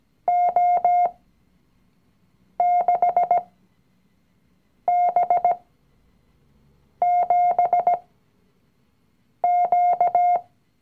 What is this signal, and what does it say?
Morse code, heard as a keyed tone, about 17 wpm, spaced as for 5 wpm, 710 Hz. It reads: O6B7Q
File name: CW-Online-Kurs / Anfänger / Lektion 019